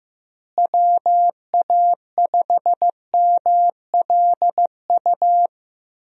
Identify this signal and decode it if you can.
Morse code, heard as a keyed tone, about 15 wpm, 700 Hz. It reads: WA5MLU
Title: USA Call Signs - 6 Characters - Repeated 3x - 15wpm